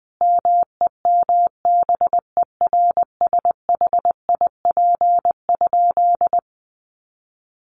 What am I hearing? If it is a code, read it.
Morse code, heard as a keyed tone, about 20 wpm, 700 Hz. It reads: MEMBERSHIP?
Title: Sentences from Top 1400 Words 20wpm